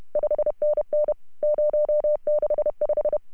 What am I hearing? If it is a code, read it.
Morse code, heard as a keyed tone, about 33 wpm, 590 Hz. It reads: 5NN 065
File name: CW / HA2OS RHOB 2017 CW